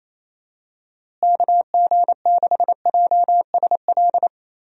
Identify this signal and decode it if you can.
Morse code, heard as a keyed tone, about 27 wpm, 700 Hz. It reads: KG6JSL
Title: USA Call Signs - 6 Characters - Repeated 3x - 28wpm